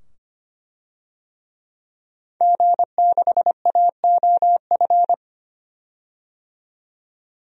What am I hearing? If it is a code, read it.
Morse code, heard as a keyed tone, about 25 wpm, 700 Hz. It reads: G6AOF
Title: DXCC-100 Call Signs - 5 Characters - 25wpm